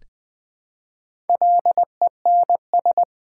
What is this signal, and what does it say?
Morse code, heard as a keyed tone, about 20 wpm, 700 Hz. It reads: LENS
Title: Build A Word - 1 Letter At A Time - 3 to 5-Letter Words - 20wpm